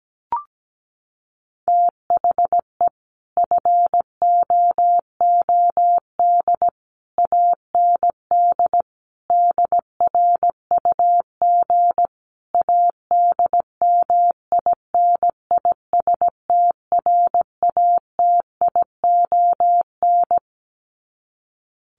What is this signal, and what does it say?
Morse code, heard as a keyed tone, about 17 wpm, 700 Hz. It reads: THE FOOD AND DRUG ADMINISTRATION
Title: Sets of 5 Words 17wpm